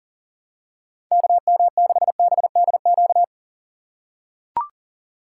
Morse code, heard as a keyed tone, about 40 wpm, 700 Hz. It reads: KM6BDQ
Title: USA Call Signs - 6 Characters - 2x Character Spacing - 40wpm